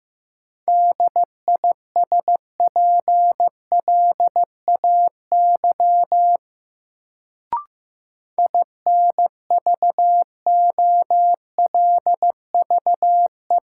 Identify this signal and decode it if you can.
Morse code, heard as a keyed tone, about 15 wpm, 700 Hz. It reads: DISPLAY INVOLVE
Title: Sentences from Top 1200 Words - New Word Drill - 15wpm